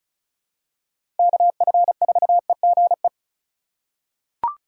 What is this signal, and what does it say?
Morse code, heard as a keyed tone, about 36 wpm, 700 Hz. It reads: KF4EGE
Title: USA Call Signs - 6 Characters - 3x Character Spacing - 35wpm